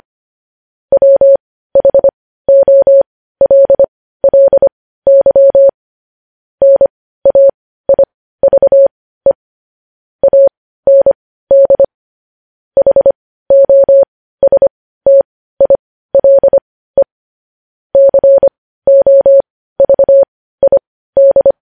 Morse code, heard as a keyed tone, about 25 wpm, spaced as for 15 wpm, 550 Hz. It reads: WHOLLY NAIVE AND HOSTILE COVID
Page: news headlines in morse code at 15 wpm